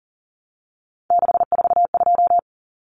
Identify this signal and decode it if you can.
Morse code, heard as a keyed tone, about 38 wpm, 700 Hz. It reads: 642
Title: Numbers - 3-Digit 40wpm